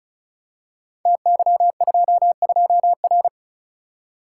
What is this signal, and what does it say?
Morse code, heard as a keyed tone, about 36 wpm, 690 Hz. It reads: TY22R
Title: Code Groups - 5 Letter:Numbers - 35wpm